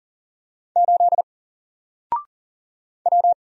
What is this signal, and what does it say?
Morse code, heard as a keyed tone, about 39 wpm, 700 Hz. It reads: 8 W
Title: Single Letter-Number Alt 40wpm